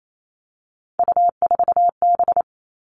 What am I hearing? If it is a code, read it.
Morse code, heard as a keyed tone, about 28 wpm, 700 Hz. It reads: U4B
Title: Three Challenging Letter-Number Combinations 28wpm